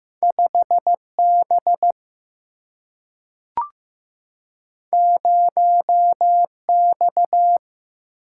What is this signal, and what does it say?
Morse code, heard as a keyed tone, about 15 wpm, 700 Hz. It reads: 5B 0X
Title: Single-Letter Single-Number 15wpm